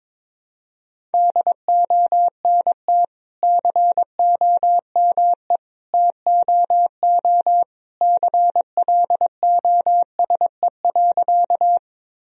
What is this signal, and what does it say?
Morse code, heard as a keyed tone, about 22 wpm, 700 Hz. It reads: DONT COME TOO CLOSE.
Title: Sentences from Top 1000 Words - Review Part 1 - 22wpm